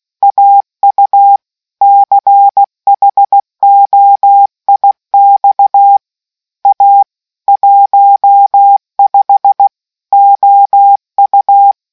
Morse code, heard as a keyed tone, about 16 wpm, 800 Hz. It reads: AU CHOIX A 15 OU